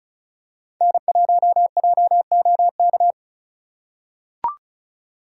Morse code, heard as a keyed tone, about 34 wpm, 700 Hz. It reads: N1JOK
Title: USA Call Signs - 5 Characters - 2x Character Spacing - 35wpm